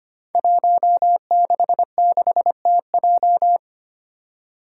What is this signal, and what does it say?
Morse code, heard as a keyed tone, about 25 wpm, 700 Hz. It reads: 166TJ
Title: Five Challenging Letter-Number Combinations 25wpm